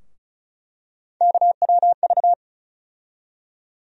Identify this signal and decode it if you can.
Morse code, heard as a keyed tone, about 36 wpm, 700 Hz. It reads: KWV
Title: USA Call Sign Postfixes - 3 Letters 35wpm